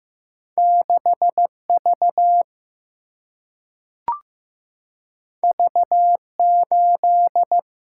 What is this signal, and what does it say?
Morse code, heard as a keyed tone, about 15 wpm, 700 Hz. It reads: 6V V8
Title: Single-Letter Single-Number 15wpm